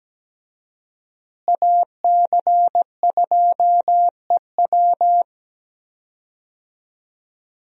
Morse code, heard as a keyed tone, about 17 wpm, 700 Hz. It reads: AC2EW
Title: USA Call Signs - 5 Characters - Repeated 3x - 17wpm